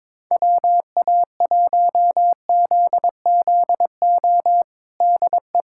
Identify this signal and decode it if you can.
Morse code, heard as a keyed tone, about 22 wpm, 700 Hz. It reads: WA1ZZO DE
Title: USA Callsign DE Callsign - 6 Characters - 22wpm